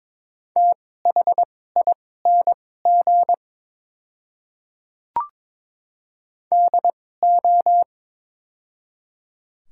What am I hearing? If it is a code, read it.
Morse code, heard as a keyed tone, about 22 wpm, spaced as for 16 wpm, 700 Hz. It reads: THING DO
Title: 2x Character Spacing - Top 100 Words - 22wpm